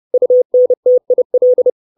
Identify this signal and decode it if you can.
Morse code, heard as a keyed tone, about 30 wpm, 500 Hz. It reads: UNTIL
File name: 632 Words 30WPM